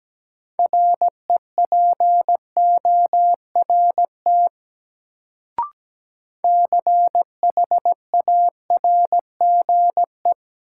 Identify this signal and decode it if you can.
Morse code, heard as a keyed tone, about 17 wpm, 700 Hz. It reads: REPORT CHARGE
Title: Sentences from Top 900 Words - New Word Drill - 17wpm